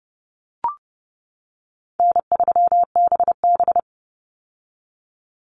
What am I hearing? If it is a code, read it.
Morse code, heard as a keyed tone, about 30 wpm, 700 Hz. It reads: N3BB
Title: Easy Call Signs - 30wpm